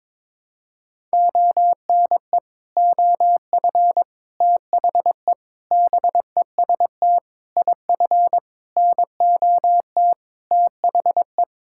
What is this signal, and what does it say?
Morse code, heard as a keyed tone, about 22 wpm, 700 Hz. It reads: ONE OF THE BEST IF NOT THE
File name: practice / 22wpm / Sets of 8 Words - 22wpm